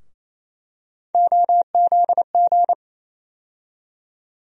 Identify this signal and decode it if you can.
Morse code, heard as a keyed tone, about 28 wpm, 700 Hz. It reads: OZG
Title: USA Call Sign Postfixes - 3 Letters 28wpm